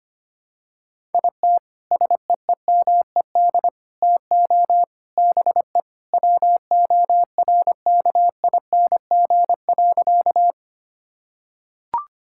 Morse code, heard as a keyed tone, about 25 wpm, 700 Hz. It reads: IT SEEMED TO BE WORKING.
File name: practice / 25wpm / Sentences with 3-Word Phrases - 25wpm